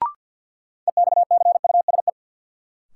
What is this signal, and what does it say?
Morse code, heard as a keyed tone, about 46 wpm, 700 Hz. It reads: EXCUSE
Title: Sentences from Top 700 Words - New Word Drill - 50wpm